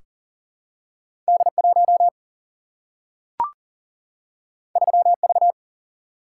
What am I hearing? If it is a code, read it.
Morse code, heard as a keyed tone, about 40 wpm, 700 Hz. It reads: D 1 3 V
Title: Single-Letter Single-Number 40wpm